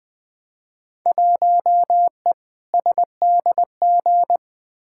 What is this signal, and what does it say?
Morse code, heard as a keyed tone, about 20 wpm, 700 Hz. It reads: 1E SDG
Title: Field Day Exchanges - 20wpm